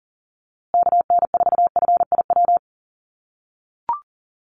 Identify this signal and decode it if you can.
Morse code, heard as a keyed tone, about 40 wpm, 700 Hz. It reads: KN4FIW
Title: USA Call Signs Repeated 2x 40wpm